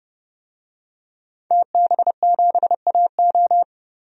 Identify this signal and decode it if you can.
Morse code, heard as a keyed tone, about 30 wpm, 700 Hz. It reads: TB7AO
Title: DXCC-100 Call Signs - 5 Characters - 30wpm